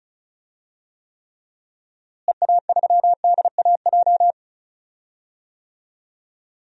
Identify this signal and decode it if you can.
Morse code, heard as a keyed tone, about 36 wpm, 700 Hz. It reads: EA3DAJ
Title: DXCC-100 Call Signs - 6 Characters - Repeated 3x - 35wpm